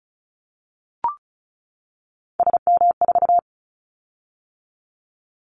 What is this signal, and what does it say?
Morse code, heard as a keyed tone, about 36 wpm, 700 Hz. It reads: SM4
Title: Callsign Prefixes for Top 25 DXCC - 35wpm